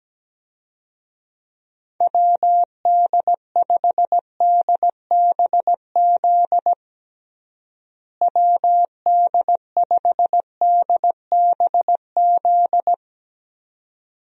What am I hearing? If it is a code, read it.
Morse code, heard as a keyed tone, about 17 wpm, 700 Hz. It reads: WD5DBZ WD5DBZ
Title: USA Call Signs - 6 Characters - Repeated 2x - 17wpm